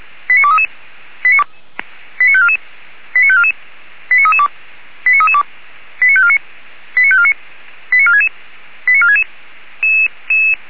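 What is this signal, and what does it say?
Five-tone selective call (ZVEI), heard as quick runs of five tones, tones of about 70 ms. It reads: 89233 89544 89544 89332 89332 89549 89549 89466 89466
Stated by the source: ZVEI five-tone call received by SDR, from the feuerwehr-orlamuende.org fire brigade server.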